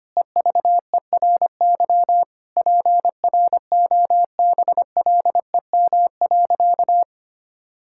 Morse code, heard as a keyed tone, about 25 wpm, 700 Hz. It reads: EVERY PROBLEM.
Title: Sentences from Top 1000 Words - Review Part 5 - 25wpm